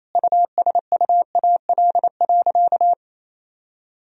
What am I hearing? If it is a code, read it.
Morse code, heard as a keyed tone, about 28 wpm, 700 Hz. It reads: USUAL.
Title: Sentences from Top 1000 Words - Review Part 2 - 28wpm